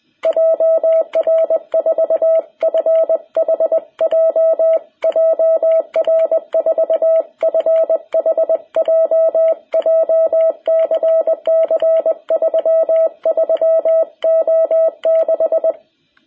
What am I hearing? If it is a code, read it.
Morse code, heard as a keyed tone, about 20 wpm, 660 Hz. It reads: JR4FHJ JR4FHJ JCC 33O6
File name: jh8cbh / 2021cw